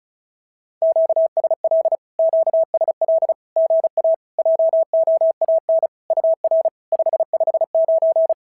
Morse code, heard as a keyed tone, about 34 wpm, 650 Hz. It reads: QSL QSL GA JOAN UR 559
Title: SOTA QSO Practice 35wpm